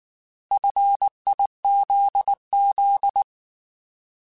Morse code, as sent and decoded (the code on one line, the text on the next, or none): ..-. .. --.. --..
FIZZ